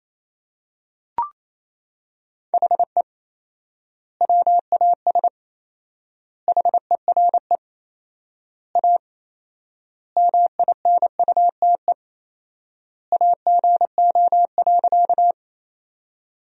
Morse code, heard as a keyed tone, about 28 wpm, 700 Hz. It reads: HE WAS HERE A MINUTE AGO.
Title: 4x Word-Spacing - Sentences from Top 500 Words - 28wpm